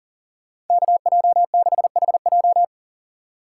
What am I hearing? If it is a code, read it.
Morse code, heard as a keyed tone, about 50 wpm, 700 Hz. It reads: KJ6HJ
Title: Speed Racing - USA 2x2 Call Signs 40wpm